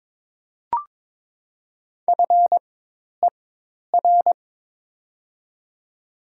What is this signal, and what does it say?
Morse code, heard as a keyed tone, about 22 wpm, spaced as for 10 wpm, 700 Hz. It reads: FER
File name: Top 20 QSO Elements - 4x Character Spacing - 22wpm